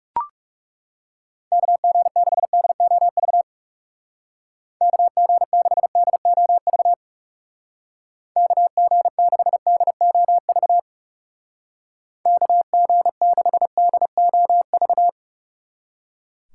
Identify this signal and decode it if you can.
Morse code, heard as a keyed tone, about 38 wpm, 700 Hz. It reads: KG6DOV KG6DOV KG6DOV KG6DOV